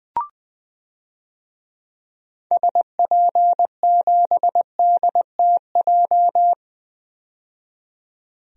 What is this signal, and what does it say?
Morse code, heard as a keyed tone, about 20 wpm, 700 Hz. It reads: SP7DTJ
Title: DXCC-100 Call Signs - 6 Characters - 20wpm